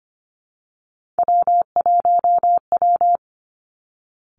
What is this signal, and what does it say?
Morse code, heard as a keyed tone, about 25 wpm, 700 Hz. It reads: W1W
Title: Three Challenging Letter-Number Combinations 25wpm